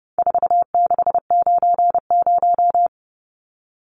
Morse code, heard as a keyed tone, about 30 wpm, 700 Hz. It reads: 4690
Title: Numbers - 4-Digit 30wpm